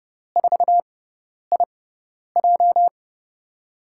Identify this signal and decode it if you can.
Morse code, heard as a keyed tone, about 30 wpm, spaced as for 10 wpm, 700 Hz. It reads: 4IJ
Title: USA Call Signs - 5 Characters - 6x Character Spacing - 30wpm